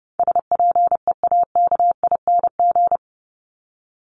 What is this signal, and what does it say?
Morse code, heard as a keyed tone, about 30 wpm, 700 Hz. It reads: SPEAKING
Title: ING Suffix - Top 100 Words with Base Words 30wpm